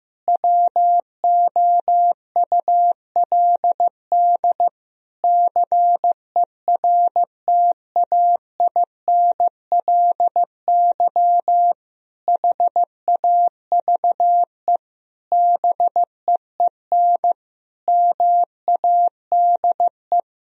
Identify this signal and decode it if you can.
Morse code, heard as a keyed tone, about 15 wpm, 700 Hz. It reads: WOULD CERTAINLY HAVE BEEN MADE
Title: Sentences from Top 1000 Words - Review Part 3 - 15wpm